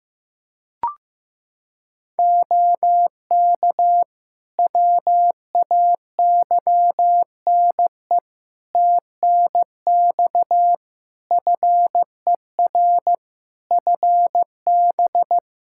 Morse code, heard as a keyed tone, about 15 wpm, 700 Hz. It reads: OK WAYNE TNX FER FB
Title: Practice QSO - Exchange 4 - Ending - 15wpm